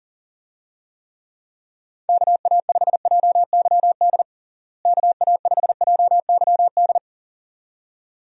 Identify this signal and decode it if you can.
Morse code, heard as a keyed tone, about 40 wpm, 700 Hz. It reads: KA5JYD KA5JYD
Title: USA Call Signs - 6 Characters - Repeated 2x - 40wpm